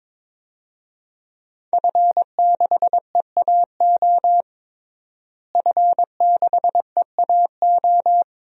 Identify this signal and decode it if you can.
Morse code, heard as a keyed tone, about 22 wpm, 700 Hz. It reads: F6EAO F6EAO
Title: DXCC-100 Call Signs - 5 Characters - Repeated 2x - 22wpm